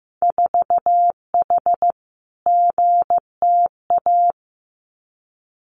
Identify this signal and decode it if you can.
Morse code, heard as a keyed tone, about 15 wpm, 700 Hz. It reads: 4H GTA
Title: Winter Field Day Exchanges Encore 15wpm